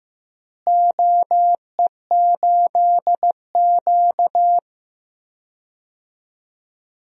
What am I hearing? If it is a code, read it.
Morse code, heard as a keyed tone, about 15 wpm, 700 Hz. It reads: OE8Q
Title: DXCC-100 Call Signs - 4 Characters - Repeated 2x - 15wpm